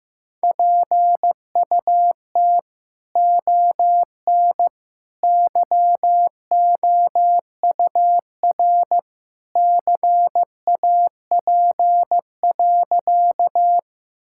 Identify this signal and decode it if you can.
Morse code, heard as a keyed tone, about 16 wpm, 700 Hz. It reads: PUT ON YOUR CAP.
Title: Sentences from Top 1600 Words 15wpm